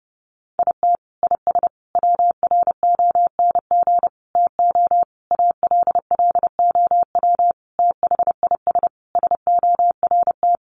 Morse code, heard as a keyed tone, about 30 wpm, 700 Hz. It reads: IT IS WRONG TO ALLOW THIS SORT
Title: Sentences from Top 1300 Words 30wpm